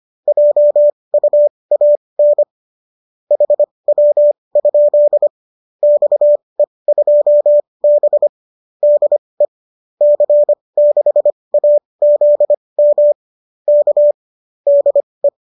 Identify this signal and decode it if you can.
Morse code, heard as a keyed tone, about 25 wpm, spaced as for 20 wpm, 590 Hz. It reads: JUAN HW? XE2B DE C6AZM K DE